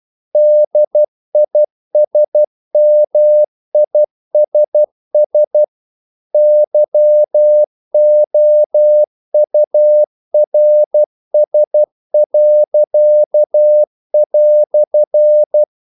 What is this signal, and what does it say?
Morse code, heard as a keyed tone, about 12 wpm, 600 Hz. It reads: DISMISS YOURS."
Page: English quotations in CW (telegraphy - morse)